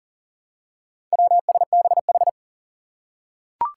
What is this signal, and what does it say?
Morse code, heard as a keyed tone, about 40 wpm, 700 Hz. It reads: WSBH
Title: Four Challenging Letter-Number Combinations 40wpm